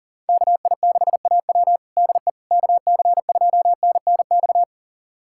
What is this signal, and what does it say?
Morse code, heard as a keyed tone, about 40 wpm, 700 Hz. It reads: KI6AW DE KC2NNX
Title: USA Callsign DE Callsign 40wpm